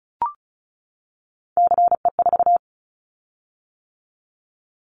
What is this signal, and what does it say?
Morse code, heard as a keyed tone, about 34 wpm, 700 Hz. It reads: CE4
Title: Callsign Prefixes for Top 100 DXCC - 35wpm